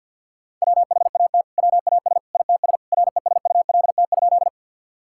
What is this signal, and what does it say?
Morse code, heard as a keyed tone, about 49 wpm, 700 Hz. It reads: WHAT WAS ITS RESULT?
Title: Sentences from Top 1000 Words - Review Part 5 - 50wpm